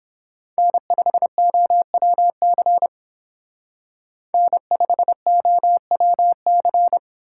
Morse code, heard as a keyed tone, about 31 wpm, 700 Hz. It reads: N5OWC N5OWC